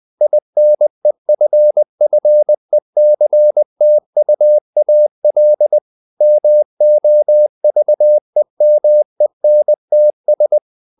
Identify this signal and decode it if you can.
Morse code, heard as a keyed tone, about 20 wpm, 600 Hz. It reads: INEFFECTUAL MOVEMENTS
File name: the old man and the sea / the old man and the sea part 1 20wpm